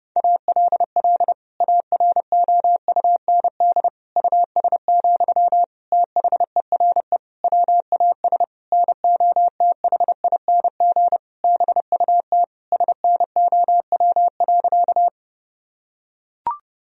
Morse code, heard as a keyed tone, about 32 wpm, 700 Hz. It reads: ALL AROUND US, THERE WAS NOTHING BUT SNOW.
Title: Sentences from Top 1000 Words - Review Part 3 - 30wpm